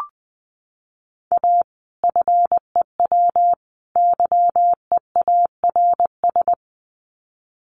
Morse code, heard as a keyed tone, about 20 wpm, 700 Hz. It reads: A FEW YEARS
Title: Sets of 3 Words 20wpm